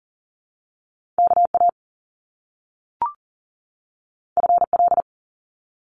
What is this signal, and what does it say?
Morse code, heard as a keyed tone, about 39 wpm, 700 Hz. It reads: KA FL
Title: Top Two Letter Combinations in Words 40wpm